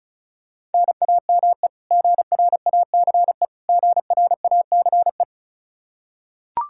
Morse code, heard as a keyed tone, about 36 wpm, 700 Hz. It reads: NAME GRACE GRACE
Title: Practice QSO - Exchange 1 - Beginning - 35wpm